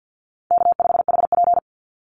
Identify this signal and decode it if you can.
Morse code, heard as a keyed tone, about 50 wpm, 700 Hz. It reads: K5HP
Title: Call Signs Repeated 3x - 50wpm